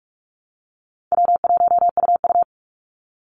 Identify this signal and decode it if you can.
Morse code, heard as a keyed tone, about 44 wpm, 700 Hz. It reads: W1UU
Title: Easy Call Signs Repeated 3x - 45wpm